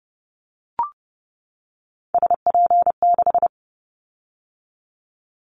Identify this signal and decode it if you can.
Morse code, heard as a keyed tone, about 30 wpm, 700 Hz. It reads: SP6
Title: Callsign Prefixes for Top 25 DXCC - 30wpm